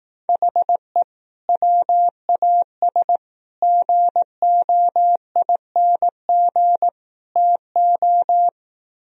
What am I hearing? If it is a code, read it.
Morse code, heard as a keyed tone, about 18 wpm, 700 Hz. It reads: HE WAS GOING TO
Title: Speed-Racing - Sets of 4 Words 15wpm